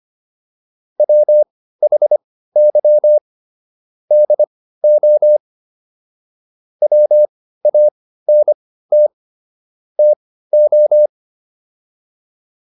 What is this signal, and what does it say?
Morse code, heard as a keyed tone, about 25 wpm, spaced as for 15 wpm, 600 Hz. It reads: WHY DO WANT TO